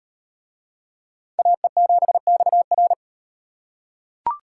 Morse code, heard as a keyed tone, about 38 wpm, 700 Hz. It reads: AE7XR